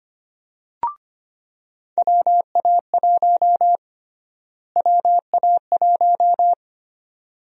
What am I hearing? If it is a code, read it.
Morse code, heard as a keyed tone, about 25 wpm, 700 Hz. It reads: WA1 WA1
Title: USA Call Sign Prefixes - Repeated 2x - 25wpm